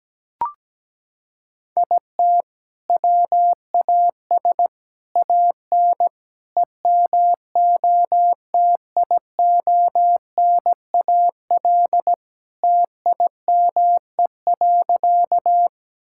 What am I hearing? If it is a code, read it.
Morse code, heard as a keyed tone, about 17 wpm, 700 Hz. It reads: IT WAS AN EMOTIONAL TIME.
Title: Sentences from Top 1000 Words - Review Part 5 - 17wpm